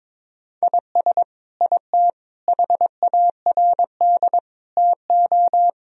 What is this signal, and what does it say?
Morse code, heard as a keyed tone, about 22 wpm, 700 Hz. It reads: IS IT HARD TO
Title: Sentences from Top 1000 Words - Review Part 3 - 22wpm